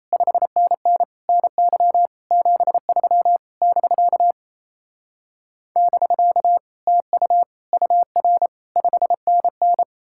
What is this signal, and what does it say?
Morse code, heard as a keyed tone, about 36 wpm, 700 Hz. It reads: 5NN NY 73 <BK> <BK> TU UR 5NN